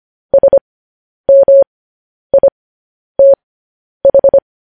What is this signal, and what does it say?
Morse code, heard as a keyed tone, about 25 wpm, spaced as for 10 wpm, 550 Hz. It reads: SMITH